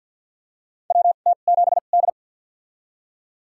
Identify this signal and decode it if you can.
Morse code, heard as a keyed tone, about 51 wpm, spaced as for 36 wpm, 700 Hz. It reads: WT7D
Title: USA Call Signs - 4 Characters - 2x Character Spacing - 50wpm